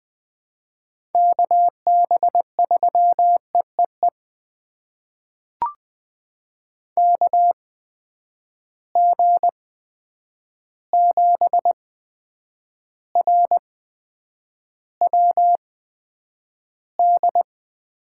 Morse code, heard as a keyed tone, about 20 wpm, spaced as for 9 wpm, 700 Hz. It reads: KB3EEE KG7RWD
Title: USA Call Signs - 6 Characters - 8x Character Spacing - 20wpm